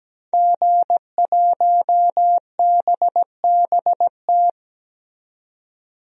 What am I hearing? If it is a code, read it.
Morse code, heard as a keyed tone, about 17 wpm, 700 Hz. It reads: G1BBT
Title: DXCC-100 Call Signs - 5 Characters - 17wpm